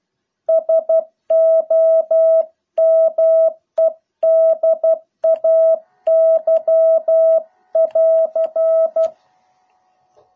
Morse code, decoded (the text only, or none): SOMEDAY <AR>